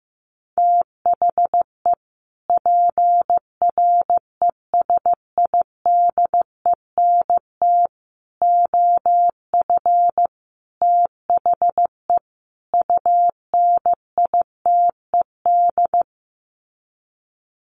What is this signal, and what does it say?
Morse code, heard as a keyed tone, about 15 wpm, 700 Hz. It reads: THE PRESIDENT OF THE UNITED
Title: Sets of 5 Words 15wpm